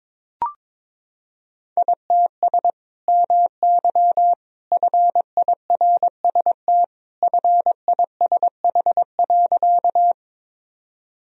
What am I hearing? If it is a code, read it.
Morse code, heard as a keyed tone, about 22 wpm, 700 Hz. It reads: ITS MY FIRST FISH.
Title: Sentences from Top 1000 Words - Review Part 1 - 22wpm